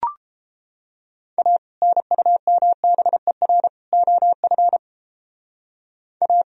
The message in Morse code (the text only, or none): A NUMBER OF A